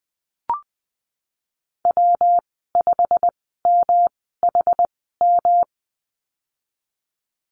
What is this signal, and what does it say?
Morse code, heard as a keyed tone, about 20 wpm, spaced as for 15 wpm, 700 Hz. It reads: W5MHM
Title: 161 Lesson 26g - Letter g - Call Signs 20wpm